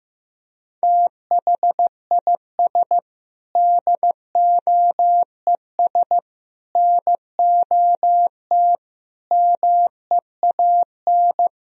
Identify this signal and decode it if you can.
Morse code, heard as a keyed tone, about 16 wpm, 700 Hz. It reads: THIS DOES NOT MEAN
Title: Sentences from Top 1000 Words - Review Part 4 - 15wpm